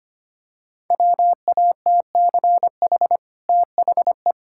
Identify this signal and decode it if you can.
Morse code, heard as a keyed tone, about 25 wpm, 690 Hz. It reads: WATCH THE
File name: Easy Sentences with Hints - 25wpm